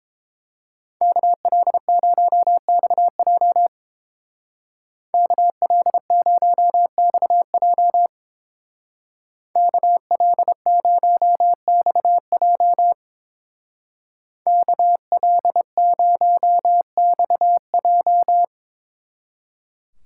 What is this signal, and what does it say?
Morse code, heard as a keyed tone, about 28 wpm, 700 Hz. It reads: KL0XJ KL0XJ KL0XJ KL0XJ